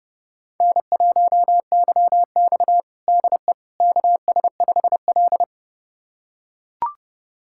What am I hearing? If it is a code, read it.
Morse code, heard as a keyed tone, about 30 wpm, 700 Hz. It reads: N1YX DE KS5L
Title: USA Callsign DE Callsign - 4 Characters - 30wpm